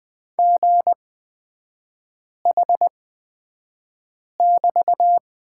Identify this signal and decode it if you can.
Morse code, heard as a keyed tone, about 20 wpm, spaced as for 5 wpm, 700 Hz. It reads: GH<BT>
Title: 250 Lesson 40e - BK - Three Characters 20wpm